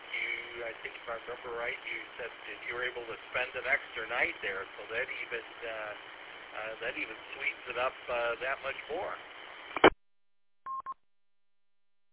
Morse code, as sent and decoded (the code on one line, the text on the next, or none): none